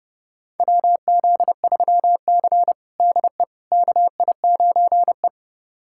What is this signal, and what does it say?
Morse code, heard as a keyed tone, about 30 wpm, 700 Hz. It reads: WZ3C DE KI9E
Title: USA Callsign DE Callsign - 4 Characters - 30wpm